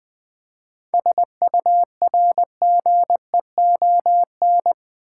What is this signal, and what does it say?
Morse code, heard as a keyed tone, about 20 wpm, 700 Hz. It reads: SURGEON
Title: 7 Letter Words - First 3 Letters Given - 20wpm